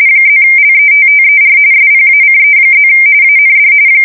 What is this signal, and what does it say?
RTTY, about 45.45 baud. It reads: +LETTER SPACE+ AND
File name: fsk170 / morkrum